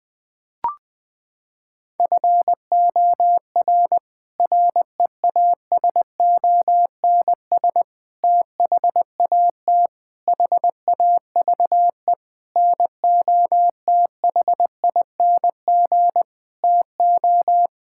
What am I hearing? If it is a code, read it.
Morse code, heard as a keyed tone, about 20 wpm, 700 Hz. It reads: FOR REASONS THAT HAVE NOTHING TO
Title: Sets of 8 Words - 20wpm